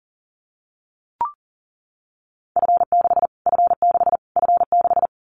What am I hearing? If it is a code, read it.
Morse code, heard as a keyed tone, about 40 wpm, 700 Hz. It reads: F6 F6 F6
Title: Callsign Prefixes for Top 25 DXCC - Repeated 3x - 40wpm